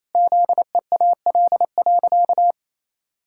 Morse code, heard as a keyed tone, about 28 wpm, 700 Hz. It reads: ZEAL.
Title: Pangram Challenge v2 - 28wpm